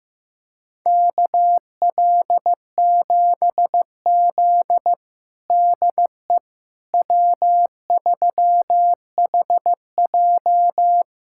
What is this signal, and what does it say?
Morse code, heard as a keyed tone, about 15 wpm, 700 Hz. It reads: KL7Z DE W3HJ
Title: USA Callsign DE Callsign - 4 Characters - 15wpm